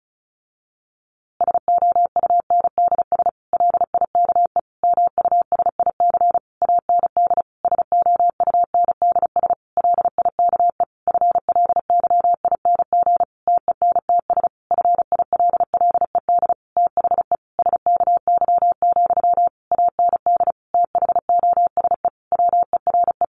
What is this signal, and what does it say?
Morse code, heard as a keyed tone, about 34 wpm, 700 Hz. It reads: SOUNDS LIKE MUSIC AND SOUNDS LIKE FLYING TENTS FILLED THE SKY, AND THOSE WERE
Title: A Sound Of Thunder 35wpm